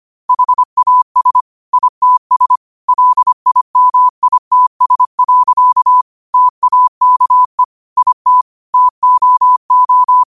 Morse code, heard as a keyed tone, about 25 wpm, 1000 Hz. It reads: HAS ITS LIMITS. TAKE IT TOO
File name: qotdfiles / 25wpm